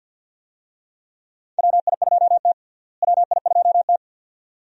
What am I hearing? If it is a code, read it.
Morse code, heard as a keyed tone, about 50 wpm, 700 Hz. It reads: WI2T WI2T
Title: USA Call Signs - 4 Characters - Repeated 2x - 50wpm